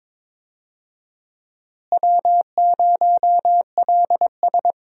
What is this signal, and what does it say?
Morse code, heard as a keyed tone, about 22 wpm, 700 Hz. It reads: W0LS
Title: USA Call Signs - 4 Characters - Repeated 2x - 22wpm